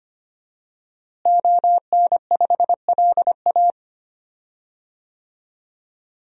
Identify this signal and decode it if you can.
Morse code, heard as a keyed tone, about 25 wpm, 700 Hz. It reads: ON5LA
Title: DXCC-100 Call Signs - 5 Characters - Repeated 3x - 25wpm